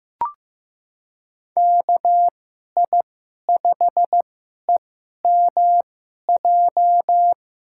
Morse code, heard as a keyed tone, about 15 wpm, spaced as for 11 wpm, 700 Hz. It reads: KI5EMJ
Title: USA Call Signs - 6 Characters - 2x Character Spacing - 15wpm